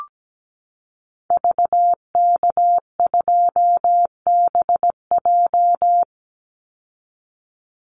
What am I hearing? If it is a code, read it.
Morse code, heard as a keyed tone, about 17 wpm, 700 Hz. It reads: VK2BJ
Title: Call Signs Repeated 2x - 17wpm